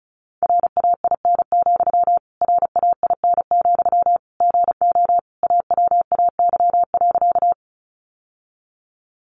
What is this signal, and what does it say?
Morse code, heard as a keyed tone, about 34 wpm, 700 Hz. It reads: RAIN, RAIN, GO AWAY.
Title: Sentences from Top 600 Words 35wpm